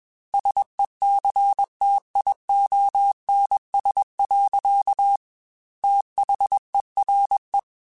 Morse code, heard as a keyed tone, about 21 wpm, 800 Hz. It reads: SECTIONS. THERE
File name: Technical Subjects / Technical Subjects-mp3s / Sw Pwr Sup @20